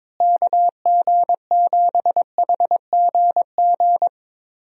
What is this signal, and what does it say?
Morse code, heard as a keyed tone, about 22 wpm, 700 Hz. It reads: KG7HGG
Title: USA Call Signs - 6 Characters - Repeated 3x - 22wpm